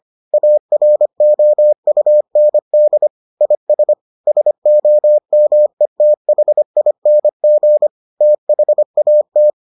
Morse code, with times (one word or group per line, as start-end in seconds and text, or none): 0.3-3.1 AROUND
3.4-3.9 IS
4.3-7.9 SOMETHING
8.2-9.5 THAT